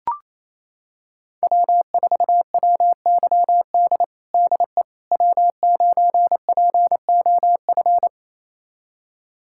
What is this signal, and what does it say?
Morse code, heard as a keyed tone, about 28 wpm, 700 Hz. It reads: W4WYD DE W9POF
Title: USA Callsign DE Callsign - 5 Characters - 28wpm